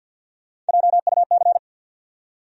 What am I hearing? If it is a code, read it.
Morse code, heard as a keyed tone, about 48 wpm, 700 Hz. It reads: JUC
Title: USA Call Sign Postfixes - 3 Letters 50wpm